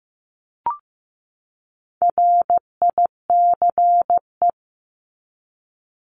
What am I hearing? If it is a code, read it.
Morse code, heard as a keyed tone, about 15 wpm, 700 Hz. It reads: RICE
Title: Top 1501-1750 Words 15wpm